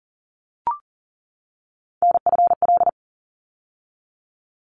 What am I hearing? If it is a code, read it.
Morse code, heard as a keyed tone, about 40 wpm, 700 Hz. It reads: NFL
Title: ARRL Sections 40wpm